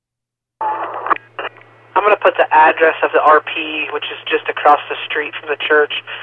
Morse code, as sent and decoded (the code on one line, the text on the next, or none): none